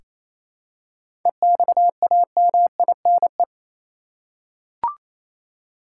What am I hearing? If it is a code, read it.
Morse code, heard as a keyed tone, about 28 wpm, 700 Hz. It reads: EXAMINE
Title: Seven Letter Words - 28wpm